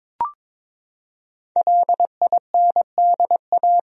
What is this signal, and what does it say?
Morse code, heard as a keyed tone, about 22 wpm, 700 Hz. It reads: LINDA
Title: USA Common Names - Max 5 Letters - 22wpm